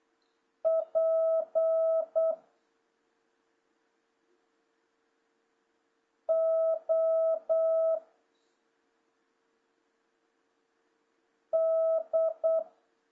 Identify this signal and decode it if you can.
Morse code, heard as a keyed tone, about 8 wpm, 650 Hz. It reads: P O D
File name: jh8cbh / 2026cw